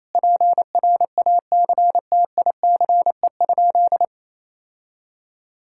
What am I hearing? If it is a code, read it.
Morse code, heard as a keyed tone, about 28 wpm, 700 Hz. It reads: PRACTICE?
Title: Sentences from Top 1000 Words - Review Part 5 - 28wpm